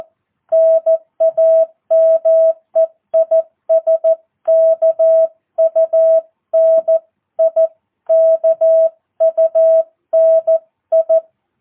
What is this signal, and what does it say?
Morse code, heard as a keyed tone, about 14 wpm, 650 Hz. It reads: NAME IS KUNI KUNI
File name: jh8cbh / 2022cw